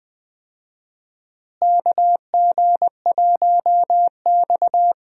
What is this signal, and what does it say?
Morse code, heard as a keyed tone, about 20 wpm, 700 Hz. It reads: KG1X
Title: USA Call Signs - 4 Characters - Repeated 2x - 20wpm